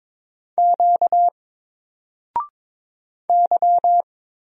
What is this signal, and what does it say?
Morse code, heard as a keyed tone, about 22 wpm, 700 Hz. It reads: Q Y
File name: Single-Letter-Number-22wpm